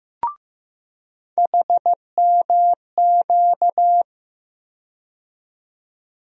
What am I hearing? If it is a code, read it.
Morse code, heard as a keyed tone, about 15 wpm, 700 Hz. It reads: HMQ
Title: USA Call Sign Postfixes - 3 Letters 15wpm